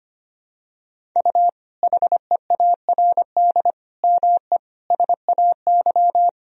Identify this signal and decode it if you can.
Morse code, heard as a keyed tone, about 25 wpm, 700 Hz. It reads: U HEARD ME SAY
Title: Sentences from Top 500 Words and Abbreviations 25wpm